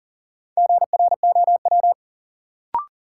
Morse code, heard as a keyed tone, about 40 wpm, 700 Hz. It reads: GROW
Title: Sentences from Top 300 Words - New Word Drill - 40wpm